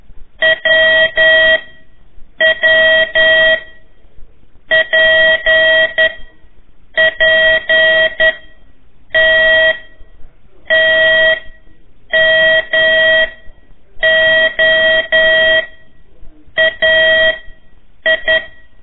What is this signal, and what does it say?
Morse code, heard as a keyed tone, about 11 wpm, spaced as for 7 wpm, 1900 Hz. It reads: WWPPT TMOAI
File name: 3 lezione CW 1 AIR